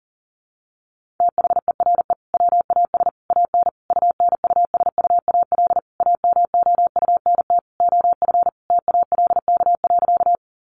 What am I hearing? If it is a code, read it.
Morse code, heard as a keyed tone, about 40 wpm, 700 Hz. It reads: THERE WAS AN UNUSUAL AMOUNT OF TALK.
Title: Sentences from Top 1400 Words 40wpm